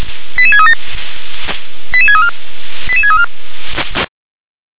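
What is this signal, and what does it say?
Five-tone selective call (ZVEI), heard as quick runs of five tones, tones of about 80 ms. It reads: 88637 88634 88634
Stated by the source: Five-tone ZVEI selective call received by SDR.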